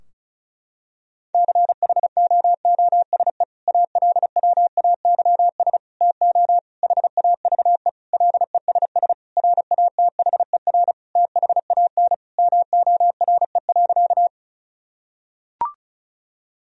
Morse code, heard as a keyed tone, about 34 wpm, 700 Hz. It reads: CHOOSE ALWAYS TO HAVE LESS RATHER THAN MORE.
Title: Sentences from Top 1000 Words - Review Part 4 - 35wpm